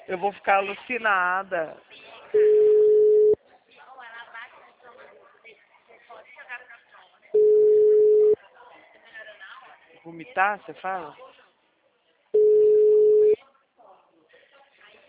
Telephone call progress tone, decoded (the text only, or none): ringback tone